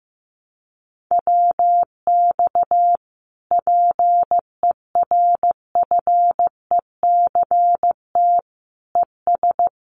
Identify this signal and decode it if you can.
Morse code, heard as a keyed tone, about 14 wpm, 700 Hz. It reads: WX PERFECT ES
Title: Common QSO Phrases Encore 15wpm